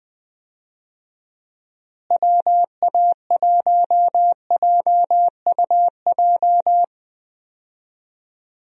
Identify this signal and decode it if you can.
Morse code, heard as a keyed tone, about 20 wpm, 700 Hz. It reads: WA1JUJ
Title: USA Call Signs - 6 Characters - Repeated 3x - 20wpm